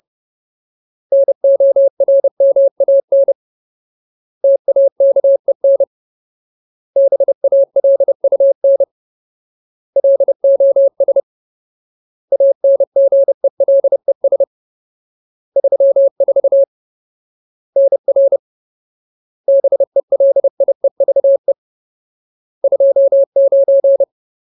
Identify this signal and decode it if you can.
Morse code, heard as a keyed tone, about 30 wpm, 550 Hz. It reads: NORMAN TAKEN BALUN LOS ANGELES 34 NR BELIEVE 29
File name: Ultimate Word List / 30WPM Ultimate